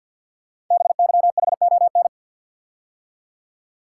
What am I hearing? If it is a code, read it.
Morse code, heard as a keyed tone, about 53 wpm, 700 Hz. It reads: BYHON
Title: Code Groups - 5 Letters - 50wpm